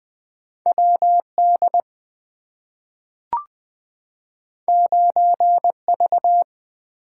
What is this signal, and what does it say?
Morse code, heard as a keyed tone, about 20 wpm, 700 Hz. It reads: W D 9 V
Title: Two Letter-Number Combinations - 20wpm